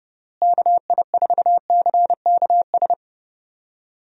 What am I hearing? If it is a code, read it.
Morse code, heard as a keyed tone, about 30 wpm, 700 Hz. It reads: KI4CKS
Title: USA Call Signs - 6 Characters - Repeated 3x - 30wpm